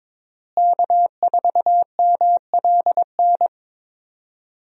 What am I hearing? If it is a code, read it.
Morse code, heard as a keyed tone, about 21 wpm, 700 Hz. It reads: K4MLN
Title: USA Call Signs - 5 Characters - Repeated 3x - 22wpm